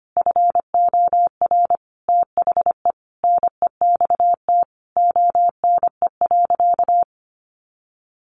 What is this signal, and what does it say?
Morse code, heard as a keyed tone, about 25 wpm, 700 Hz. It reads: FOR THE NEXT ONE.
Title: Sentences from Top 400 Words 25wpm